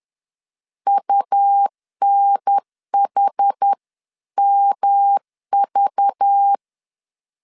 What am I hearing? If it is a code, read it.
Morse code, heard as a keyed tone, about 10 wpm, 800 Hz. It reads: UNHMV